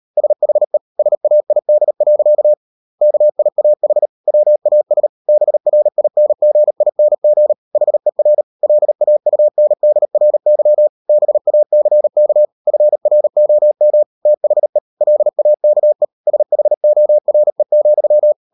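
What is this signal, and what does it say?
Morse code, heard as a keyed tone, about 38 wpm, 600 Hz. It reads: SHE SAID. KIAH WAS BRINGING HER LAUNDRY BACK FROM THE LAKE SHORE,
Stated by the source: Morse code practice at 35–39 wpm sending English text of Ken Follett's novel Never, from CW med Gnister.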